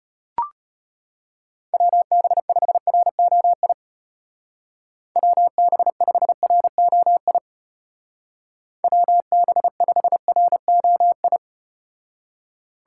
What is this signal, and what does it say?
Morse code, heard as a keyed tone, about 33 wpm, 700 Hz. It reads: WB5ROI WB5ROI WB5ROI